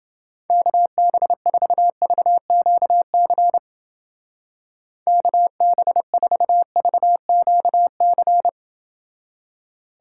Morse code, heard as a keyed tone, about 30 wpm, 700 Hz. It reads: KB4VQC KB4VQC